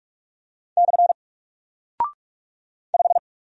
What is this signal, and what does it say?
Morse code, heard as a keyed tone, about 44 wpm, 700 Hz. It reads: / 5
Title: Single Character (Letter:Number:Punctuation:Prosign) Alt 45wpm